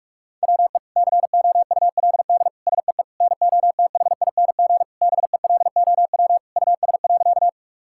Morse code, heard as a keyed tone, about 41 wpm, 700 Hz. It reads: WE COULD SEE NOTHING BELOW US.
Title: Sentences from Top 1000 Words - Review Part 5 - 45wpm